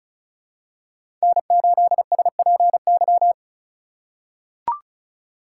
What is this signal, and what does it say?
Morse code, heard as a keyed tone, about 34 wpm, 700 Hz. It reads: N8SPY
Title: USA Call Signs - 5 Characters - 3x Character Spacing - 35wpm